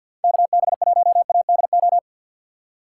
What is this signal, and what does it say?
Morse code, heard as a keyed tone, about 62 wpm, 700 Hz. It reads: KB1ADO
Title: Speed Racing - USA 2x3 Call Signs 50wpm